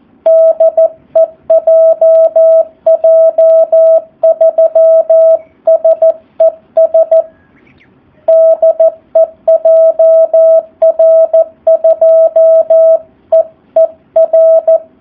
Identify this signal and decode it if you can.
Morse code, heard as a keyed tone, about 14 wpm, 650 Hz, with radio noise. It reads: DE JJ3SES DE JR2EER
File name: jh8cbh / 2019cw / SP04- w